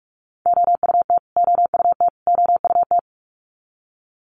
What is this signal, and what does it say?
Morse code, heard as a keyed tone, about 44 wpm, 700 Hz. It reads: OUT OUT OUT
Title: Top 500 Words Repeated 45wpm